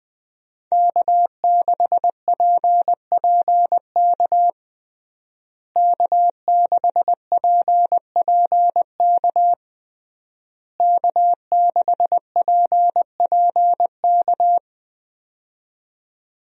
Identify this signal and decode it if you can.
Morse code, heard as a keyed tone, about 21 wpm, 700 Hz. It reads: K6PPK K6PPK K6PPK
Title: USA Call Signs - 5 Characters - Repeated 3x - 20wpm